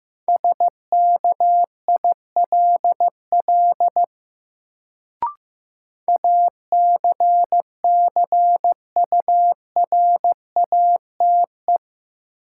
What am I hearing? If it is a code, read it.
Morse code, heard as a keyed tone, about 15 wpm, 700 Hz. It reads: SKILL ACCURATE
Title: Sentences from Top 1200 Words - New Word Drill Alt - 15wpm